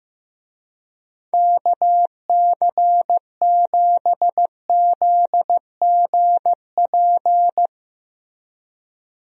Morse code, heard as a keyed tone, about 15 wpm, 700 Hz. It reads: KC7ZGP
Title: USA Call Signs - 6 Characters - Repeated 2x - 15wpm